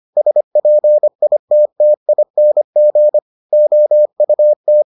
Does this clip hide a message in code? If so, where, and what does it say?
Yes: Morse code from 0.2 to 4.8 s, heard as a keyed tone, about 25 wpm, 600 Hz. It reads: SPITTING OUT